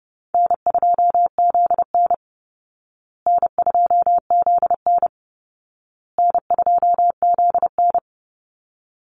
Morse code, heard as a keyed tone, about 30 wpm, 700 Hz. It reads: N2ZN N2ZN N2ZN
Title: Easy Call Signs Repeated 3x - 30wpm